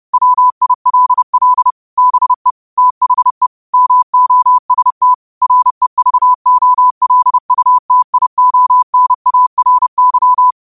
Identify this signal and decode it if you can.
Morse code, heard as a keyed tone, about 30 wpm, 1000 Hz. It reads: WILL BE THE MOST REVOLUTIONARY
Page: 30 WPM Morse code quotes